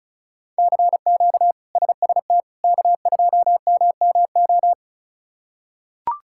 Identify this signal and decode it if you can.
Morse code, heard as a keyed tone, about 34 wpm, 700 Hz. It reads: CQ SST K2MMO
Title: K1USN SST QSO Practice - 35wpm